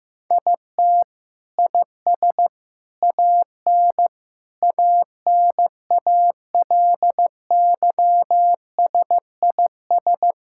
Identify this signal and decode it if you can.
Morse code, heard as a keyed tone, about 15 wpm, 700 Hz. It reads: IT IS AN ANALYSIS
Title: Sentences from Top 1000 Words - Review Part 5 - 15wpm